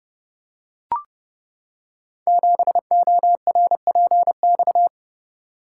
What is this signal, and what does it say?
Morse code, heard as a keyed tone, about 30 wpm, 700 Hz. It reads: 7ORPX
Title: Code Groups - 5 Letter:Numbers - 30wpm